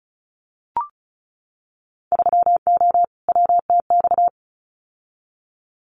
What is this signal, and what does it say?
Morse code, heard as a keyed tone, about 36 wpm, 700 Hz. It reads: 3O WTX
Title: Winter Field Day Exchanges Encore 35wpm